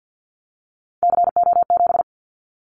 Morse code, heard as a keyed tone, about 48 wpm, 700 Hz. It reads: CO7
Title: Callsign Prefixes for Top 100 DXCC - 50wpm